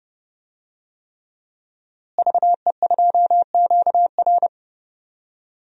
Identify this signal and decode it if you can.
Morse code, heard as a keyed tone, about 30 wpm, 700 Hz. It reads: VE2QR
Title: DXCC-100 Call Signs - 5 Characters - Repeated 3x - 30wpm